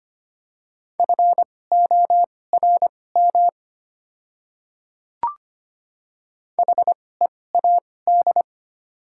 Morse code, heard as a keyed tone, about 25 wpm, spaced as for 18 wpm, 700 Hz. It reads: FORM HEAD
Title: 2x Character Spacing - Top 300 Words - 25wpm